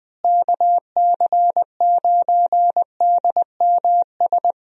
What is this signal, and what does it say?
Morse code, heard as a keyed tone, about 21 wpm, 700 Hz. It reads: KC9DMS
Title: USA Call Signs - 6 Characters - Repeated 3x - 20wpm